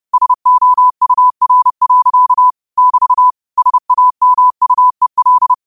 Morse code, heard as a keyed tone, about 30 wpm, 1000 Hz. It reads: SOUR. <BT> SAMUEL